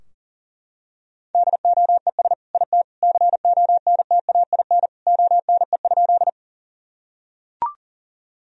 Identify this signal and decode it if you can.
Morse code, heard as a keyed tone, about 39 wpm, 700 Hz. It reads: DOES IT CONTAIN ONE?
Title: Sentences from Top 1000 Words - Review Part 2 - 40wpm